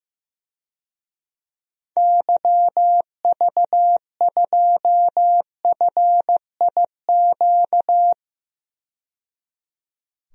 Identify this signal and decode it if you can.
Morse code, heard as a keyed tone, about 15 wpm, 700 Hz. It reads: YV2FIQ
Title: DXCC-100 Call Signs - 6 Characters - 15wpm